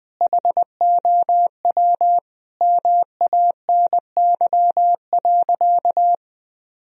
Morse code, heard as a keyed tone, about 21 wpm, 700 Hz. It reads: HOW MANY.
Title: Sentences from Top 1000 Words - Review Part 1 - 20wpm